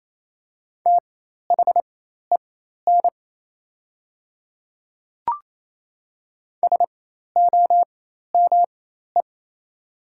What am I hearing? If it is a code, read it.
Morse code, heard as a keyed tone, about 28 wpm, spaced as for 13 wpm, 700 Hz. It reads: THEN SOME
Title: 4x Character Spacing - Top 100 Words - 28wpm